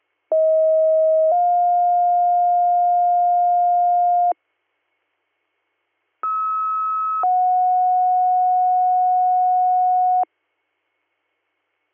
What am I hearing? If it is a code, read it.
Two-tone page, heard as a one-second tone, then a three-second tone, about 640 then 710 Hz, about 1290 then 730 Hz. It is heard from 0.3 to 10.2 s.